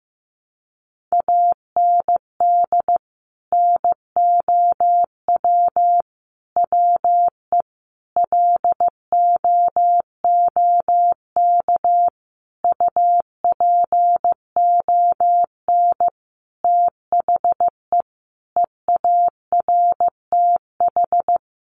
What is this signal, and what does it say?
Morse code, heard as a keyed tone, about 15 wpm, 700 Hz. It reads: AND NOW WE LOOK UPON THE EARTH
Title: Sentences from Top 1200 Words 15wpm